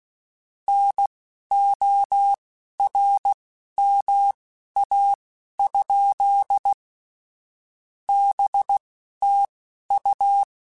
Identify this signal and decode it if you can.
Morse code, heard as a keyed tone, about 16 wpm, spaced as for 11 wpm, 800 Hz. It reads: NORMA? BTU
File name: QSOs / QSOs-mp3s / WD0FBT 1@10